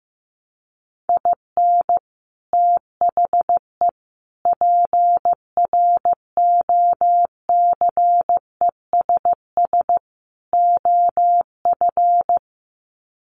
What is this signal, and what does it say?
Morse code, heard as a keyed tone, about 14 wpm, 700 Hz. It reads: IN THE PROCESS OF
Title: Sets of 4 Words 15wpm